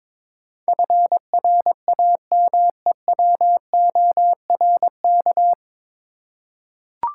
Morse code, heard as a keyed tone, about 22 wpm, 700 Hz. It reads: FRAMEWORK
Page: Morse Code Ninja practice 22wpm